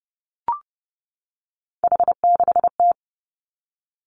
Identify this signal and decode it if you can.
Morse code, heard as a keyed tone, about 30 wpm, 700 Hz. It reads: H6T
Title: Three Challenging Letter-Number Combinations 30wpm